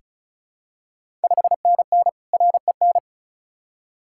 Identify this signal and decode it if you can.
Morse code, heard as a keyed tone, about 36 wpm, 700 Hz. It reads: 5NN REN
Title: Ontario QSO Party Exchange 35wpm